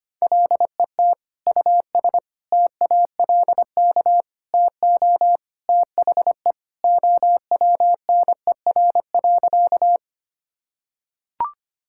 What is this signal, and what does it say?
Morse code, heard as a keyed tone, about 25 wpm, 700 Hz. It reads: LET US TALK TO THE OWNER.
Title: Sentences from Top 1600 Words 25wpm